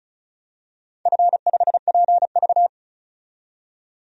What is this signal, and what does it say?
Morse code, heard as a keyed tone, about 36 wpm, 700 Hz. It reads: F5PV
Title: DXCC-100 Call Signs - 4 Characters - 35wpm